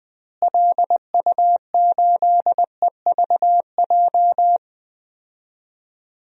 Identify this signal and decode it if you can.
Morse code, heard as a keyed tone, about 20 wpm, 700 Hz. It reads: LU8EVJ
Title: DXCC-100 Call Signs - 6 Characters - Repeated 2x - 20wpm